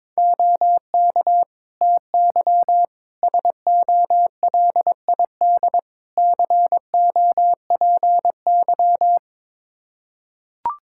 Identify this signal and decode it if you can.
Morse code, heard as a keyed tone, about 22 wpm, 700 Hz. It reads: OK TY SOLID COPY
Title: K1USN SST QSO Practice 22wpm